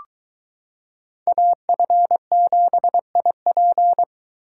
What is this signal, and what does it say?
Morse code, heard as a keyed tone, about 22 wpm, 700 Hz. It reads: AF7IP